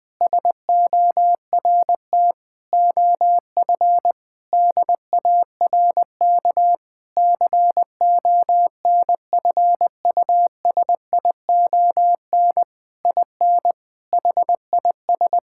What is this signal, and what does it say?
Morse code, heard as a keyed tone, about 20 wpm, 700 Hz. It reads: SORT OF DARK CONFUSION IN HIS
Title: Sentences from Top 1600 Words 20wpm